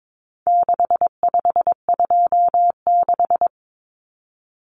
Morse code, heard as a keyed tone, about 22 wpm, 700 Hz. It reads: 6526
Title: Numbers - 4-Digit 22wpm